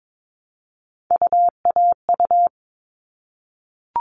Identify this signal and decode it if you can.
Morse code, heard as a keyed tone, about 22 wpm, 700 Hz. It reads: UAU
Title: All Cut Numbers - 3-Digit 22wpm